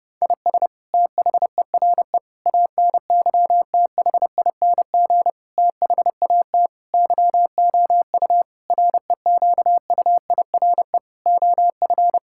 Morse code, heard as a keyed tone, about 32 wpm, 700 Hz. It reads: IS THERE ANYTHING THAT YOU REQUIRE OF
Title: Sentences from Top 1000 Words - Review Part 6 - 30wpm